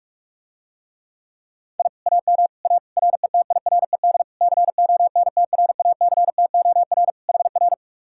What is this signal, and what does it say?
Morse code, heard as a keyed tone, about 47 wpm, 700 Hz. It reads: I AM A RETIRED CONTRACTOR HR